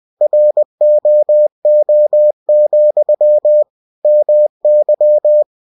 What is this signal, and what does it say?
Morse code, heard as a keyed tone, about 20 wpm, 600 Hz. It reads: ROO, MY
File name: cw-books / winnie-the-pooh / winnie-20wpm-page-0066